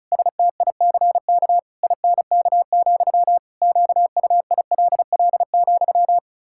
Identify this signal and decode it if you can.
Morse code, heard as a keyed tone, about 34 wpm, 700 Hz. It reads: STICK INK, QUILL,
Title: Pangram Challenge v2 - 35wpm